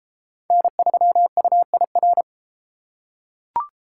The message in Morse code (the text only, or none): N3UIR